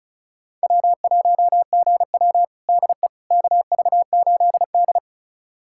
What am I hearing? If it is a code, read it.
Morse code, heard as a keyed tone, about 36 wpm, 700 Hz. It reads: W1GW DE KV8D
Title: USA Callsign DE Callsign - 4 Characters - 35wpm